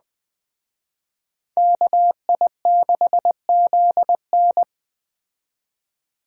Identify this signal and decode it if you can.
Morse code, heard as a keyed tone, about 20 wpm, 700 Hz. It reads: KI6ZN
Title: Speed Racing - USA 2x2 Call Signs 17wpm